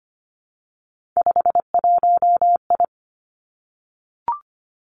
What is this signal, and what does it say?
Morse code, heard as a keyed tone, about 25 wpm, 700 Hz. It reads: 51I
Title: Three Challenging Letter-Number Combinations 25wpm